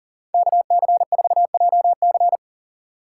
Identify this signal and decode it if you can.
Morse code, heard as a keyed tone, about 38 wpm, 700 Hz. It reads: KC4JC
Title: USA Call Signs - 6x Character Spacing - 40wpm